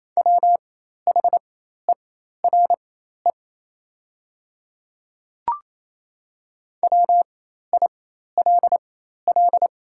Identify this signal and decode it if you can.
Morse code, heard as a keyed tone, about 28 wpm, spaced as for 13 wpm, 700 Hz. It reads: WHERE WILL
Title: 4x Character Spacing - Top 500 Words - 28wpm